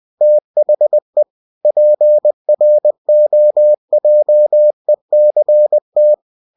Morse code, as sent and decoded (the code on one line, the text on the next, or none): - .... / .--. .-. --- .--- . -.-. -
THE PROJECT